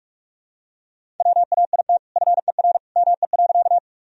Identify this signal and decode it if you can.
Morse code, heard as a keyed tone, about 45 wpm, 700 Hz. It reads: WAIT FER ME.
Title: Sentences from Top 500 Words and Abbreviations 45wpm